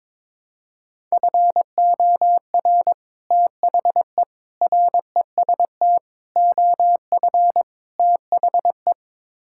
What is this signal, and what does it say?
Morse code, heard as a keyed tone, about 21 wpm, 700 Hz. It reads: FOR THE REST OF THE
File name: practice / 22wpm / Build A Phrase - 5-Word Phrases - 22wpm